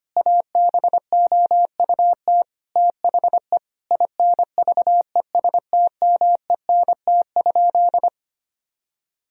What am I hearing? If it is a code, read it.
Morse code, heard as a keyed tone, about 25 wpm, 700 Hz. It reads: ABOUT THE INVESTMENT?
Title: Sentences from Top 1000 Words - Review Part 4 - 25wpm